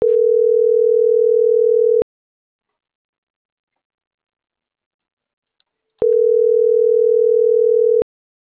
Ringback tone (North America), 440 + 480 Hz.